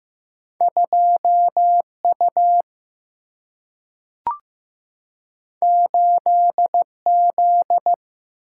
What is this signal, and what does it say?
Morse code, heard as a keyed tone, about 15 wpm, 700 Hz. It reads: 2U 8Z
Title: Single-Letter Single-Number 15wpm